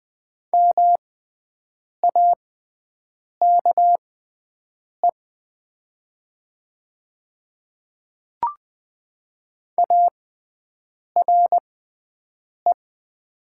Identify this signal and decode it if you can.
Morse code, heard as a keyed tone, about 20 wpm, spaced as for 7 wpm, 700 Hz. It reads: MAKE ARE
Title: 6x Character Spacing - Top 100 Words - 20wpm